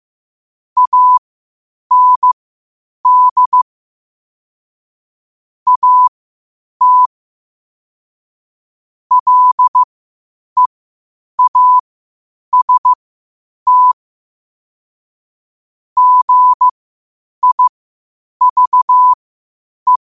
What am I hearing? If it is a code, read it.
Morse code, heard as a keyed tone, about 15 wpm, spaced as for 8 wpm, 1000 Hz. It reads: AND AT LEAST GIVE
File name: qotdfiles / 8wpm